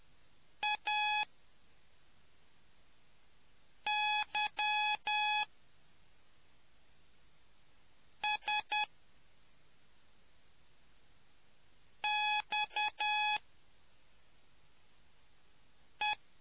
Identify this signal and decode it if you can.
Morse code, heard as a keyed tone, about 10 wpm, spaced as for 3 wpm, 840 Hz. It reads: AYSXE